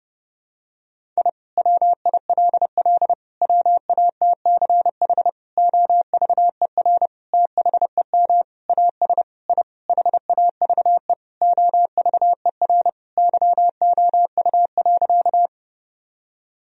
Morse code, heard as a keyed tone, about 32 wpm, 700 Hz. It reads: I WILL WATCH OVER THEM AS I HAVE OVER YOU.
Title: Sentences from Top 1000 Words - Review Part 1 - 30wpm